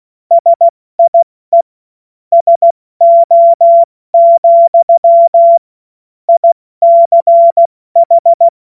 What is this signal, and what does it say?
Morse code, heard as a keyed tone, about 16 wpm, spaced as for 14 wpm, 680 Hz. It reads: SIE SO, ICH